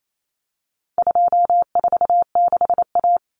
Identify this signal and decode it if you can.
Morse code, heard as a keyed tone, about 28 wpm, 700 Hz. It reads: 246A
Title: Common Cut Numbers - 4-Digit 28wpm